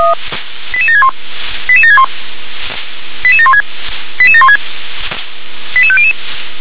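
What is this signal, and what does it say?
Five-tone selective call (ZVEI), heard as quick runs of five tones, tones of about 70 ms. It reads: 88761 88761 88716 88716 88500
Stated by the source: ZVEI five-tone selective call received on SDR.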